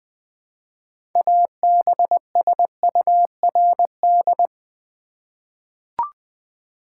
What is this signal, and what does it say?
Morse code, heard as a keyed tone, about 21 wpm, 700 Hz. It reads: ABSURD
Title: 6 Letter Words - First 3 Letters Given - 20wpm